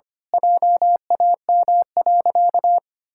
Morse code, heard as a keyed tone, about 25 wpm, 700 Hz. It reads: JAM.
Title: Pangram Challenge v2 - 25wpm